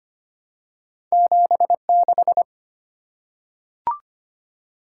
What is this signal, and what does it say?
Morse code, heard as a keyed tone, about 25 wpm, 700 Hz. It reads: 76
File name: Two Letter-Number Combinations - 25wpm